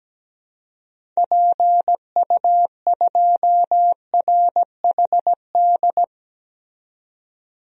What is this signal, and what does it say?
Morse code, heard as a keyed tone, about 17 wpm, 700 Hz. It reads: PU2RHD
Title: DXCC-100 Call Signs - 6 Characters - 17wpm